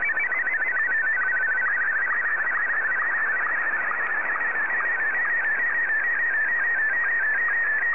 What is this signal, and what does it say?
SSTV (slow-scan television), Martin 1.
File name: sstv / M1-ZL2TJM